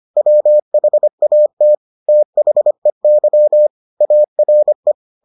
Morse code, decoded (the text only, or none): WHAT THEY ARE